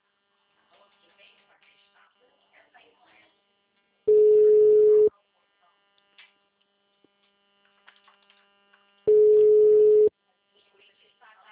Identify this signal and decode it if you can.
Ringback tone (Europe, 425 Hz), with speech.